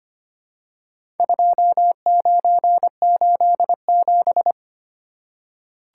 Morse code, heard as a keyed tone, about 25 wpm, 700 Hz. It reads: 2987
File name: practice / 25wpm / Four Letter-Number Combinations - 25wpm